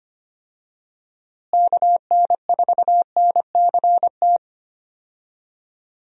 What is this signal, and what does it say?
Morse code, heard as a keyed tone, about 25 wpm, 700 Hz. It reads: KN4NCT
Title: USA Call Signs - 6 Characters - Repeated 2x - 25wpm